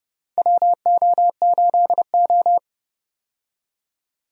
Morse code, heard as a keyed tone, about 30 wpm, 700 Hz. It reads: WO8O
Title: USA Call Signs - 4 Characters - 30wpm